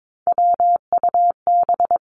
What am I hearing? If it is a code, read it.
Morse code, heard as a keyed tone, about 22 wpm, 700 Hz. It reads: WUB
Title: Three Challenging Letter-Number Combinations 22wpm